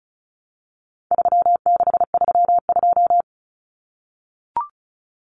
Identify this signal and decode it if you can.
Morse code, heard as a keyed tone, about 34 wpm, 700 Hz. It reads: 3632
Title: Common Cut Numbers - 4-Digit 35wpm